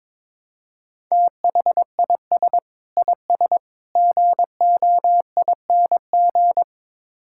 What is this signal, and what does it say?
Morse code, heard as a keyed tone, about 22 wpm, 700 Hz. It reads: THIS IS GOING
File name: practice / 22wpm / Build A Phrase - 5-Word Phrases - 22wpm